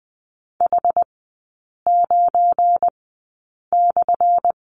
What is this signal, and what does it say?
Morse code, heard as a keyed tone, about 20 wpm, spaced as for 8 wpm, 700 Hz. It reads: H9/
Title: 184 Lesson 30e - Letter b - Three Characters 20wpm